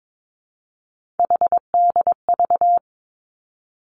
Morse code, heard as a keyed tone, about 22 wpm, 700 Hz. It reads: HDV
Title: Three Challenging Letter-Number Combinations 22wpm